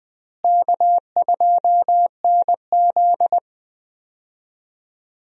Morse code, heard as a keyed tone, about 20 wpm, 700 Hz. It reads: K2NZ
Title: USA Call Signs - 4 Characters - Repeated 3x - 20wpm